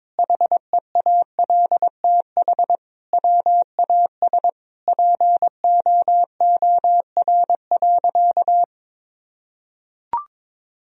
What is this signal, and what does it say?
Morse code, heard as a keyed tone, about 22 wpm, 700 Hz. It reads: HEALTH WAS POOR.
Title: Sentences from Top 1000 Words - Review Part 3 - 22wpm